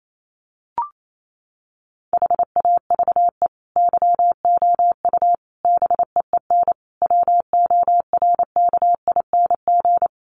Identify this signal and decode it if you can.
Morse code, heard as a keyed tone, about 28 wpm, 700 Hz. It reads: HAVE YOU BEEN WORKING
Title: Sentences from Top 700 Words 28wpm